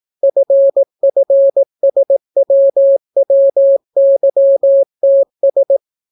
Morse code, heard as a keyed tone, about 18 wpm, 550 Hz. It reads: FFSWWYTS